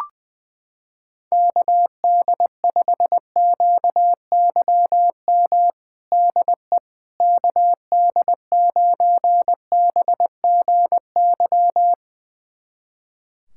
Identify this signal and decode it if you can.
Morse code, heard as a keyed tone, about 20 wpm, 700 Hz. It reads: KD5QYM DE KD9BGY
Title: USA Callsign DE Callsign - 6 Characters - 20wpm